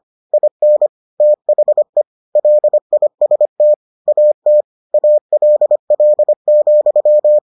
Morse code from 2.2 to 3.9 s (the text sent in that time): LIST